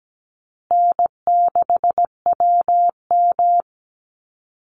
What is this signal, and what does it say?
Morse code, heard as a keyed tone, about 17 wpm, 700 Hz. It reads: N6WM
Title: Easy Call Signs Repeated 3x - 17wpm